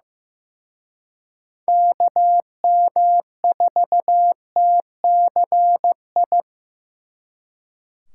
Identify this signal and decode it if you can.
Morse code, heard as a keyed tone, about 18 wpm, 700 Hz. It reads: KM4TCI
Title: Speed Racing - USA 2x3 Call Signs 15wpm